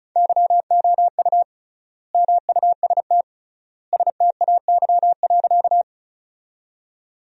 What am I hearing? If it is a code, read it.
Morse code, heard as a keyed tone, about 36 wpm, 700 Hz. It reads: YOU MUST STAY.
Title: 3x Word-Spacing - Sentences from Top 500 Words - 35wpm